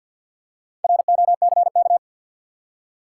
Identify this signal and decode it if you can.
Morse code, heard as a keyed tone, about 48 wpm, 700 Hz. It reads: ROCK
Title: Four Letter Words - 50wpm